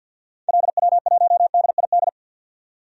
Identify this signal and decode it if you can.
Morse code, heard as a keyed tone, about 52 wpm, 700 Hz. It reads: LW1DID
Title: DXCC-100 Call Signs - 6 Characters - 50wpm